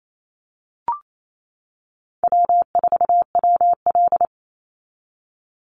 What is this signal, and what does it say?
Morse code, heard as a keyed tone, about 28 wpm, 700 Hz. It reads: W4WL
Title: USA Call Signs 28wpm